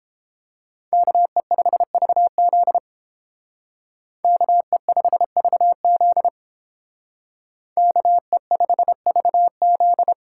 Morse code, heard as a keyed tone, about 30 wpm, 700 Hz. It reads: KE5VZ KE5VZ KE5VZ